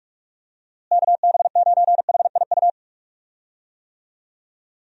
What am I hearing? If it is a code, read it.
Morse code, heard as a keyed tone, about 46 wpm, 700 Hz. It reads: KB9HIU
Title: USA Call Signs - 6 Characters - Repeated 3x - 45wpm